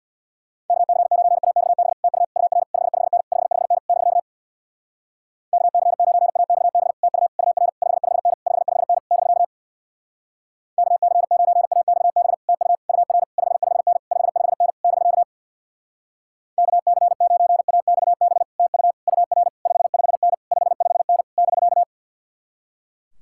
Morse code, heard as a keyed tone, about 65 wpm, 700 Hz. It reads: KC9AXB TU UR 55N 55N <BK> KC9AXB TU UR 55N 55N <BK> KC9AXB TU UR 55N 55N <BK> KC9AXB TU UR 55N 55N <BK>